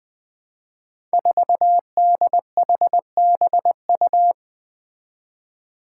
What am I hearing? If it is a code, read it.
Morse code, heard as a keyed tone, about 20 wpm, 700 Hz. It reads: 4DHBU
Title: Five Challenging Letter-Number Combinations 20wpm